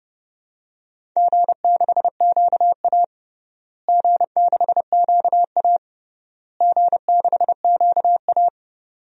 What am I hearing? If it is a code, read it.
Morse code, heard as a keyed tone, about 30 wpm, 700 Hz. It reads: G6QA G6QA G6QA
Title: DXCC-100 Call Signs - 4 Characters - Repeated 3x - 30wpm